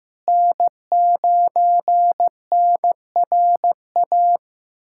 Morse code, heard as a keyed tone, about 15 wpm, 700 Hz. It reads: N9NRA
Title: USA Callsign DE Callsign - 5 Characters - 15wpm